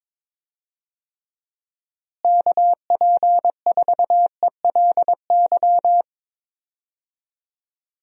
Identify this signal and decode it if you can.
Morse code, heard as a keyed tone, about 22 wpm, 700 Hz. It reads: KP4ELY
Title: USA Call Signs - 6 Characters - Repeated 3x - 22wpm